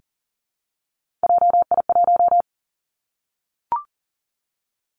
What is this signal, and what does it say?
Morse code, heard as a keyed tone, about 38 wpm, 700 Hz. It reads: JI1
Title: Three Challenging Letter-Number Combinations 40wpm